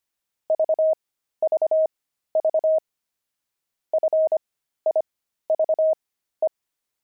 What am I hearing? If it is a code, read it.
Morse code, heard as a keyed tone, about 25 wpm, spaced as for 13 wpm, 630 Hz. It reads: VVV FIVE
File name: pangram-fs-13-cs-25-jet